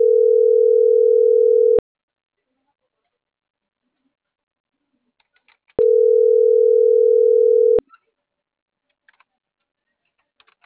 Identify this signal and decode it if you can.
Ringback tone (North America), 440 + 480 Hz.